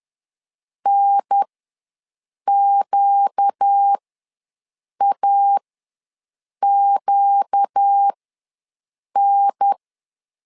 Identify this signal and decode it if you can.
Morse code, heard as a keyed tone, about 10 wpm, spaced as for 6 wpm, 800 Hz. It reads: NQAQN
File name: LEZIONI / 4 Q ws 10wpm 2306